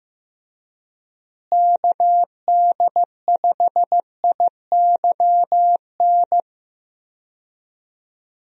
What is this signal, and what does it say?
Morse code, heard as a keyed tone, about 15 wpm, 700 Hz. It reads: KD5IYN
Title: USA Call Signs - 6 Characters - Repeated 3x - 15wpm